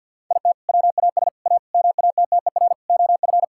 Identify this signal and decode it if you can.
Morse code, heard as a keyed tone, about 50 wpm, 700 Hz. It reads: IT WAS A MATTER OF
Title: Sentences from Top 1000 Words - Review Part 5 - 50wpm